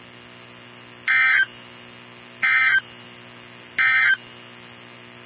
Emergency Alert System (SAME) tones, heard as harsh buzzing bursts. It is heard from 1.0 to 4.2 s.
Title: NOAA Weather Radio 162.550